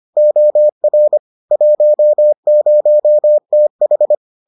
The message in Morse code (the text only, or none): OR 10TH